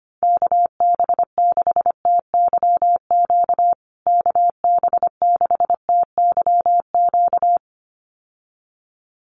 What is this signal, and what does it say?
Morse code, heard as a keyed tone, about 25 wpm, 700 Hz. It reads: KB6TYQ KB6TYQ
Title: USA Call Signs Repeated 2x 25wpm